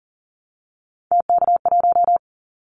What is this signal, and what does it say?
Morse code, heard as a keyed tone, about 42 wpm, 690 Hz. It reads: TK1
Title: Callsign Prefixes for Top 100 DXCC - 40wpm